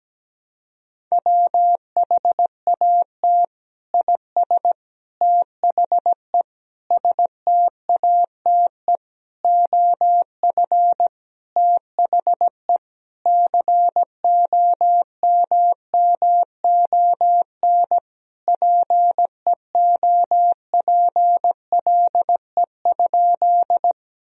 Morse code, heard as a keyed tone, about 17 wpm, 700 Hz. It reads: WHAT IS THE STATE OF THE COMMON PEOPLE?
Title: Sentences from Top 1000 Words - Review Part 3 - 17wpm